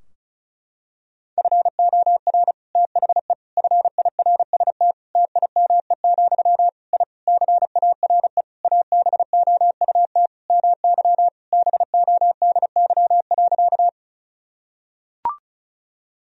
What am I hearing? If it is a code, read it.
Morse code, heard as a keyed tone, about 36 wpm, 700 Hz. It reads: FOR THE FIRST TIME, I CARE ABOUT MY BODY.
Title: Sentences from Top 1000 Words - Review Part 2 - 35wpm